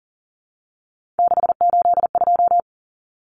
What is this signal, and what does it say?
Morse code, heard as a keyed tone, about 40 wpm, 700 Hz. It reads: 682
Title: Common Cut Numbers - 3-Digit 40wpm